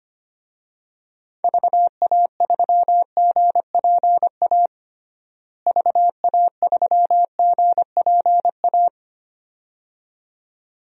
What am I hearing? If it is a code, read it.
Morse code, heard as a keyed tone, about 25 wpm, 700 Hz. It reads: VA3GPA VA3GPA
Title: DXCC-100 Call Signs - 6 Characters - Repeated 2x - 25wpm